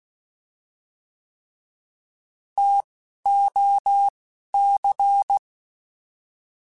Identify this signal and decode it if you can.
Morse code, heard as a keyed tone, about 16 wpm, 800 Hz. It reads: T O C
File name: T Prefixes / T mp3s